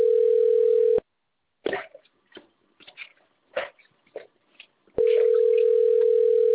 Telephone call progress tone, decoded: ringback tone